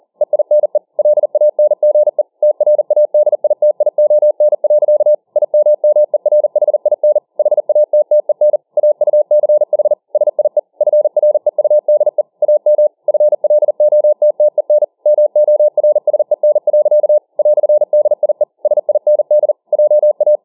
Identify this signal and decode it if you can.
Morse code, heard as a keyed tone, about 40 wpm, 600 Hz. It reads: EINE LANGE TRADITION. IMMERHIN HATTEN AUCH SIE FREUDE AM FLOTTEN MORSEN. "DIE SIND JA